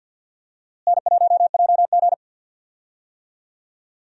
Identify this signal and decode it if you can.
Morse code, heard as a keyed tone, about 50 wpm, 700 Hz. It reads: N1JG
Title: USA Call Signs - 4 Characters - Repeated 3x - 50wpm